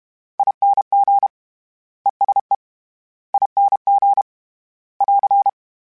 Morse code, heard as a keyed tone, about 32 wpm, 800 Hz. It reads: ING ESE ING <AR>
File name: General / General-mp3s / ESE ING @30